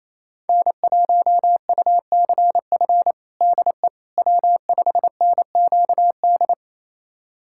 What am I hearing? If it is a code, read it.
Morse code, heard as a keyed tone, about 27 wpm, 700 Hz. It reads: N1UCF DE W5NQD
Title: USA Callsign DE Callsign - 5 Characters - 28wpm